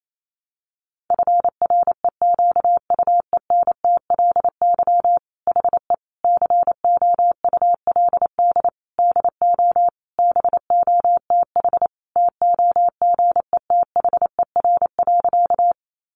Morse code, heard as a keyed tone, about 28 wpm, 700 Hz. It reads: FREQUENTLY HE COULD DO BOTH TOGETHER.
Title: Sentences from Top 1300 Words 28wpm